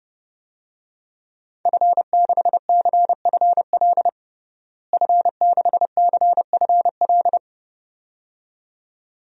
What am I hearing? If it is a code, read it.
Morse code, heard as a keyed tone, about 30 wpm, 700 Hz. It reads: F6CFL F6CFL
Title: DXCC-100 Call Signs - 5 Characters - Repeated 2x - 30wpm